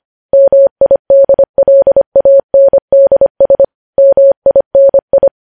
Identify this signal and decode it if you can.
Morse code, heard as a keyed tone, about 25 wpm, 550 Hz. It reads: MIDLANDS MINI
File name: podcast / News headlines-1x-25x25